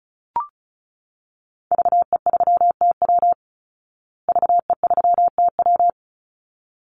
Morse code, heard as a keyed tone, about 34 wpm, 700 Hz. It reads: VE3TW VE3TW
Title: Call Signs Repeated 3x - 35wpm